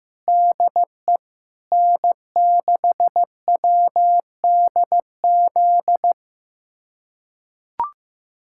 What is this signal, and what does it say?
Morse code, heard as a keyed tone, about 16 wpm, 700 Hz. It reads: DE N6WDZ
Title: USA Callsign DE Callsign - 5 Characters - 15wpm